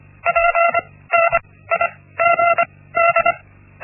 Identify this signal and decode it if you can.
Morse code, heard as a keyed tone, about 25 wpm, spaced as for 17 wpm, 680 Hz. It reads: PNIGD
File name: Linksmakalnis-1